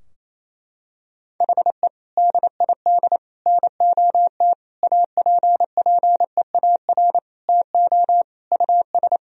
Morse code, heard as a keyed tone, about 28 wpm, 700 Hz. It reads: HE DID NOT APPEAR TO US
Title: Sentences from Top 1000 Words - Review Part 2 - 28wpm